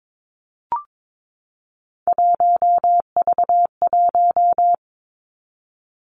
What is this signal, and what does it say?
Morse code, heard as a keyed tone, about 22 wpm, 700 Hz. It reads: 1V1
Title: Three Challenging Letter-Number Combinations 22wpm